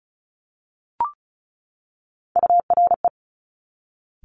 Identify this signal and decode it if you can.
Morse code, heard as a keyed tone, about 36 wpm, 700 Hz. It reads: URE
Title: Top Three Letter Combinations in Words 35wpm